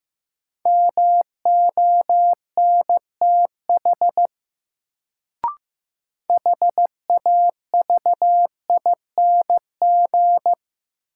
Morse code, heard as a keyed tone, about 15 wpm, 700 Hz. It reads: MONTH HAVING
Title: Sets of 5 Words Encore - New Word Drill - 15wpm